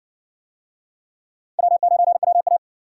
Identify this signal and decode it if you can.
Morse code, heard as a keyed tone, about 59 wpm, 700 Hz. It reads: W9WA